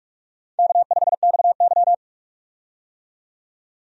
Morse code, heard as a keyed tone, about 48 wpm, 700 Hz. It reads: K5XY
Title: USA Call Signs - 4 Characters - 45wpm